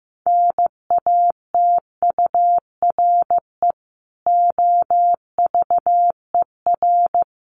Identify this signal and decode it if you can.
Morse code, heard as a keyed tone, about 15 wpm, 700 Hz. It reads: NATURE OVER
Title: Sentences from Top 1500 Words 15wpm